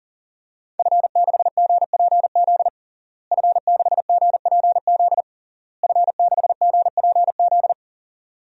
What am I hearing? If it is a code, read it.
Morse code, heard as a keyed tone, about 40 wpm, 700 Hz. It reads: F6GPZ F6GPZ F6GPZ
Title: DXCC-100 Call Signs - 5 Characters - Repeated 3x - 40wpm